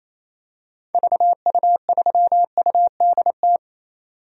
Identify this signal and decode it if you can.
Morse code, heard as a keyed tone, about 28 wpm, 700 Hz. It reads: VU3UDT
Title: DXCC-100 Call Signs - 6 Characters - 28wpm